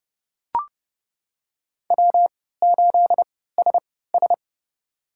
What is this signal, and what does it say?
Morse code, heard as a keyed tone, about 30 wpm, spaced as for 17 wpm, 700 Hz. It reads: W8SS
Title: USA Call Signs - 3x Character Spacing - 30wpm